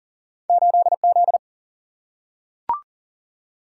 Morse code, heard as a keyed tone, about 40 wpm, 700 Hz. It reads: 8 Z
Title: Single-Letter Single-Number 40wpm